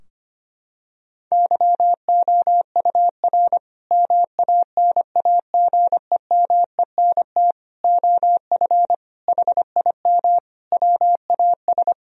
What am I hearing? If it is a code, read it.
Morse code, heard as a keyed tone, about 25 wpm, 700 Hz. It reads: YOUR MANAGEMENT OF HIM WAS
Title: Sentences from Top 1000 Words - Review Part 4 - 25wpm